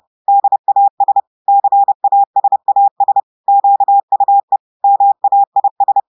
Morse code, heard as a keyed tone, about 30 wpm, 800 Hz. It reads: DAS CASAS QUE MAIS
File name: audios / portugais / portugais texte en clair 1 vitesse 30 mots minutes